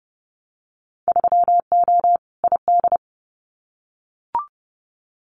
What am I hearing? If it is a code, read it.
Morse code, heard as a keyed tone, about 31 wpm, 700 Hz. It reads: 3O ID
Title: Winter Field Day Exchanges 30wpm